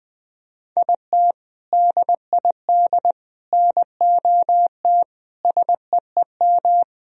Morse code, heard as a keyed tone, about 20 wpm, 700 Hz. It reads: IT DID NOT SEEM
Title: Sentences from Top 1000 Words - Review Part 1 - 20wpm